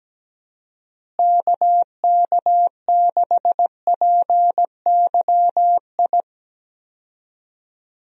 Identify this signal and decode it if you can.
Morse code, heard as a keyed tone, about 17 wpm, 700 Hz. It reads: KK6PYI
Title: USA Call Signs - 6 Characters - Repeated 3x - 17wpm